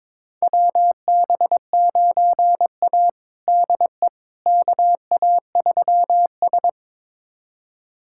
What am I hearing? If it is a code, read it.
Morse code, heard as a keyed tone, about 21 wpm, 700 Hz. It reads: WB9A DE KA3S
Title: USA Callsign DE Callsign - 4 Characters - 22wpm